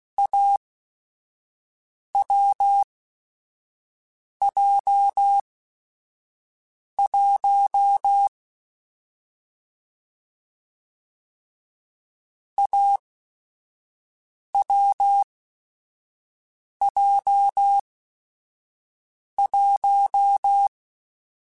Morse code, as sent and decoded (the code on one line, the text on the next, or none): .- .-- .--- .---- / .- .-- .--- .----
AWJ1 AWJ1